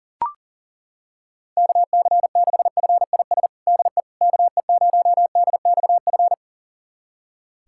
Morse code, heard as a keyed tone, about 37 wpm, 700 Hz. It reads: KC6FIS DE KE0DXF
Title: USA Callsign DE Callsign - 6 Characters - 40wpm